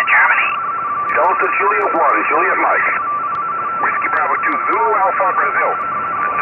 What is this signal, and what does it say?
Two-tone teleprinter signal, about 170 Hz shift, with speech.